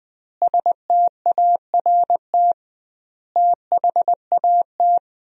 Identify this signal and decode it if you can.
Morse code, heard as a keyed tone, about 20 wpm, 700 Hz. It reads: START THAT
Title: 2x Word-Spacing - Sentences from Top 300 Words - 20wpm